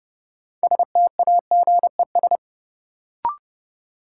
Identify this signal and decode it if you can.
Morse code, heard as a keyed tone, about 30 wpm, 700 Hz. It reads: STAGES
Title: Sets of 5 Words Encore - New Word Drill - 30wpm